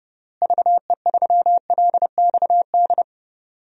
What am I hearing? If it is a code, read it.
Morse code, heard as a keyed tone, about 30 wpm, 700 Hz. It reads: VE3LXD
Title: DXCC-100 Call Signs - 6 Characters - Repeated 3x - 30wpm